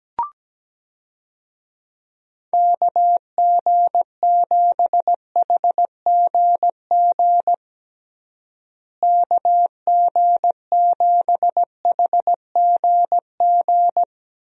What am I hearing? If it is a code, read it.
Morse code, heard as a keyed tone, about 17 wpm, 700 Hz. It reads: KG7HGG KG7HGG
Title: USA Call Signs - 6 Characters - Repeated 3x - 17wpm